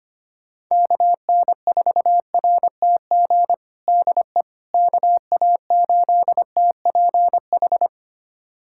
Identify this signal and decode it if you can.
Morse code, heard as a keyed tone, about 25 wpm, 700 Hz. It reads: KN4RTG DE KA8TPH
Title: USA Callsign DE Callsign - 6 Characters - 25wpm